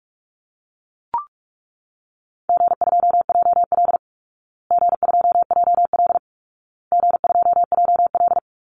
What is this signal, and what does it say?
Morse code, heard as a keyed tone, about 44 wpm, 700 Hz. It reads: G2JL G2JL G2JL
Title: Easy Call Signs Repeated 3x - 45wpm